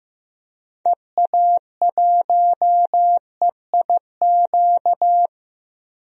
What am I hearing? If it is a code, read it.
Morse code, heard as a keyed tone, about 15 wpm, 700 Hz. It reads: EA1EIQ
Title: DXCC-100 Call Signs - 6 Characters - Repeated 3x - 15wpm